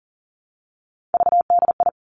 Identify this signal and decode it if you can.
Morse code, heard as a keyed tone, about 38 wpm, 700 Hz. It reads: VDI
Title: Three Challenging Letter-Number Combinations 40wpm